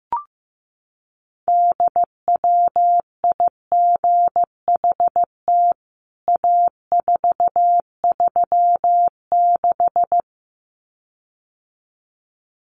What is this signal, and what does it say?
Morse code, heard as a keyed tone, about 15 wpm, 700 Hz. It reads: DWIGHT A436
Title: CWT Exchange 15wpm